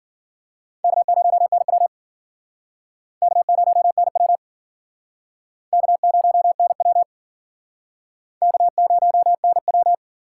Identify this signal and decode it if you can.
Morse code, heard as a keyed tone, about 54 wpm, 700 Hz. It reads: K0NW K0NW K0NW K0NW